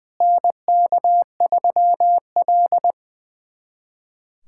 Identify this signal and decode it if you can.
Morse code, heard as a keyed tone, about 20 wpm, 700 Hz. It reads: NK3L
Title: USA Callsign DE Callsign - 4 Characters - 20wpm